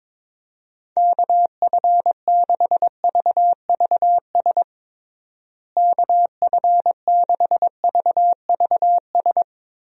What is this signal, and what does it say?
Morse code, heard as a keyed tone, about 22 wpm, 700 Hz. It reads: KF6VVS KF6VVS
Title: USA Call Signs - 6 Characters - Repeated 3x - 22wpm